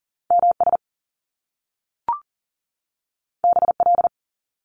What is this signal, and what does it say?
Morse code, heard as a keyed tone, about 39 wpm, 700 Hz. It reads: MS BL
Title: Top Two Letter Combinations in Words 40wpm